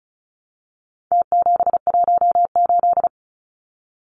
Morse code, heard as a keyed tone, about 34 wpm, 700 Hz. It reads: T718
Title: Numbers - Contest Style - 4-Digit 35wpm